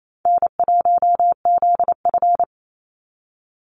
Morse code, heard as a keyed tone, about 28 wpm, 700 Hz. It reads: N1ZF
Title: NA SOTA Chaser Callsigns 28wpm